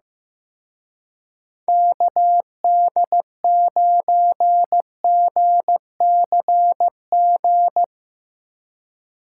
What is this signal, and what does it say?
Morse code, heard as a keyed tone, about 17 wpm, 700 Hz. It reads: KD9GCG
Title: Speed Racing - USA 2x3 Call Signs 15wpm